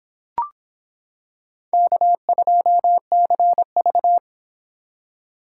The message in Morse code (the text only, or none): K2CV